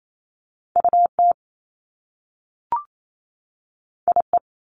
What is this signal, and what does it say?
Morse code, heard as a keyed tone, about 28 wpm, 700 Hz. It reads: UT IE
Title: Top Two Letter Combinations in Words 28wpm